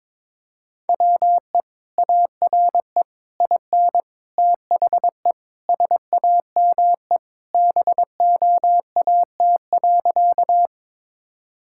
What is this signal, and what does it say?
Morse code, heard as a keyed tone, about 22 wpm, 700 Hz. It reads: WE ARE IN THE SAME BOAT.
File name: practice / 22wpm / Sentences with 3-Word Phrases - 22wpm